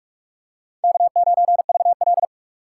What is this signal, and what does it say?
Morse code, heard as a keyed tone, about 45 wpm, 700 Hz. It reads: K9VL